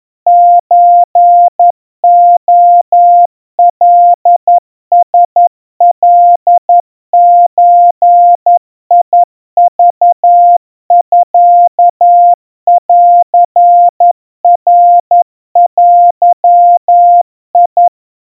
Morse code (Japanese Wabun code), heard as a keyed tone, about 11 wpm, 700 Hz. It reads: ソレカラカゾクミンナデ
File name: kawa / Morse sample / kamyu 40 cpm 700 Hz Crispy